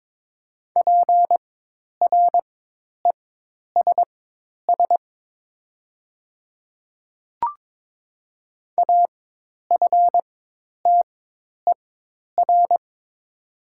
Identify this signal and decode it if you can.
Morse code, heard as a keyed tone, about 22 wpm, spaced as for 10 wpm, 700 Hz. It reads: PRESS AFTER
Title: 4x Character Spacing - Top 300 Words - 22wpm